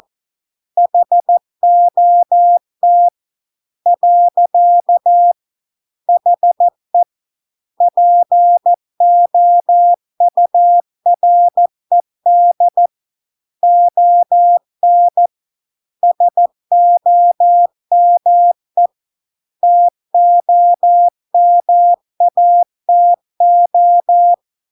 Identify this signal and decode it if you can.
Morse code, heard as a keyed tone, about 14 wpm, 700 Hz. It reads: HOT . HE POURED ON SOME TOMATO